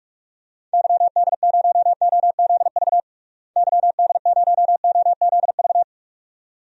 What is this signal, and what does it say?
Morse code, heard as a keyed tone, about 46 wpm, 700 Hz. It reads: YD0OZV YD0OZV
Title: DXCC-100 Call Signs - 6 Characters - Repeated 2x - 45wpm